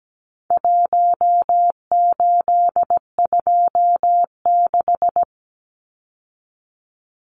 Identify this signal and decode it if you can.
Morse code, heard as a keyed tone, about 17 wpm, 700 Hz. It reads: 1826
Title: Numbers - Contest Style - 4-Digit 17wpm